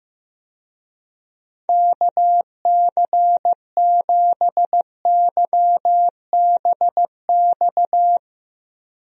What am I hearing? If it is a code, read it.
Morse code, heard as a keyed tone, about 15 wpm, 700 Hz. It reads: KC7YBX
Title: USA Call Signs - 6 Characters - Repeated 3x - 15wpm